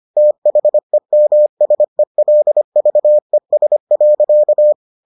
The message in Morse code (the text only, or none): THEMSELVES.